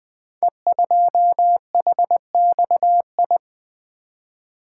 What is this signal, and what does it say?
Morse code, heard as a keyed tone, about 20 wpm, 700 Hz. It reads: E2HXI
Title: Code Groups - 5 Letter:Numbers - 20wpm